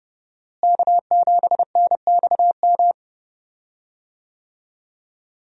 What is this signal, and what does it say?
Morse code, heard as a keyed tone, about 30 wpm, 700 Hz. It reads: K7NXM
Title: USA Call Signs - 5 Characters - Repeated 2x - 30wpm